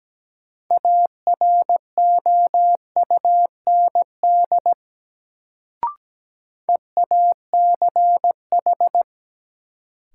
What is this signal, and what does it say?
Morse code, heard as a keyed tone, about 17 wpm, 700 Hz. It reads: AROUND EACH
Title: Sets of 2 Words - New Word Drill - 17wpm